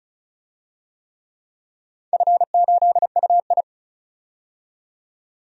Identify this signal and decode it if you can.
Morse code, heard as a keyed tone, about 36 wpm, 700 Hz. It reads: F8UI
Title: DXCC-100 Call Signs - 4 Characters - 35wpm